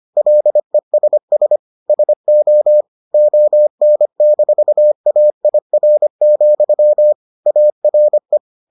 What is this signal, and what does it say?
Morse code, heard as a keyed tone, about 25 wpm, 600 Hz. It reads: LESS SO ON-AIR, ARE